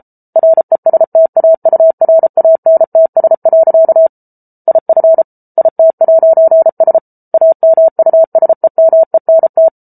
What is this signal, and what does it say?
Morse code, heard as a keyed tone, about 34 wpm, 650 Hz. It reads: RESTAURANTS. IF IT'S AMUSEMENT